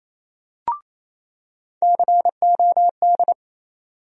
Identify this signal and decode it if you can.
Morse code, heard as a keyed tone, about 28 wpm, 700 Hz. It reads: COD
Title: USA Call Sign Postfixes - 3 Letters 28wpm